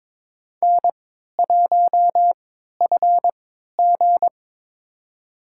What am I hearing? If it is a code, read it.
Morse code, heard as a keyed tone, about 22 wpm, spaced as for 12 wpm, 700 Hz. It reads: N1FG
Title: USA Call Signs - 4 Characters - 3x Character Spacing - 22wpm